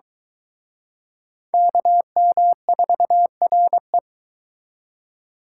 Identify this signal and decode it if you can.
Morse code, heard as a keyed tone, about 22 wpm, 700 Hz. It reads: KM4RE